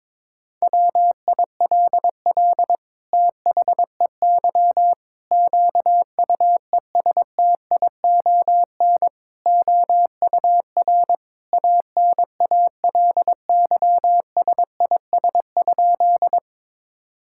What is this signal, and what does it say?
Morse code, heard as a keyed tone, about 22 wpm, 700 Hz. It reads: WILL THEY QUESTION OUR ANALYSIS?
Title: Sentences from Top 1000 Words - Review Part 5 - 22wpm